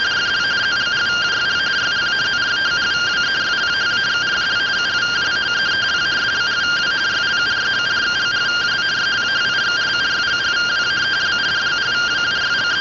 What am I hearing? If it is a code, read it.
Two-tone teleprinter signal, about 170 Hz shift.
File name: findings / S ss-portrait.gen-16 rtty rx